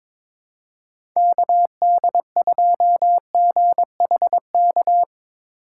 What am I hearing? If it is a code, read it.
Morse code, heard as a keyed tone, about 22 wpm, 700 Hz. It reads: KD2GHK
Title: USA Call Signs - 3x Character Spacing - 22wpm